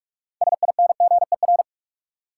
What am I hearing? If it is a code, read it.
Morse code, heard as a keyed tone, about 42 wpm, 700 Hz. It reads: SINGER
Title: Six Letter Words - 45wpm